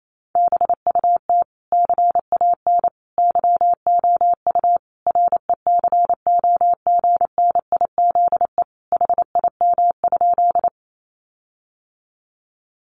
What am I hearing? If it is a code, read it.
Morse code, heard as a keyed tone, about 28 wpm, 700 Hz. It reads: BUT CAN YOU RECOGNIZE HIM?
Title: Sentences from Top 1400 Words 28wpm